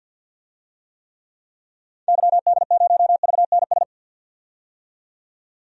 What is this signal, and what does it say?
Morse code, heard as a keyed tone, about 47 wpm, 700 Hz. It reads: YD0VNS
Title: DXCC-100 Call Signs - 6 Characters - 50wpm